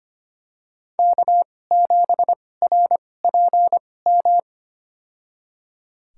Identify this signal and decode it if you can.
Morse code, heard as a keyed tone, about 25 wpm, spaced as for 18 wpm, 700 Hz. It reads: K7RPM
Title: USA Call Signs - 5 Characters - 2x Character Spacing - 25wpm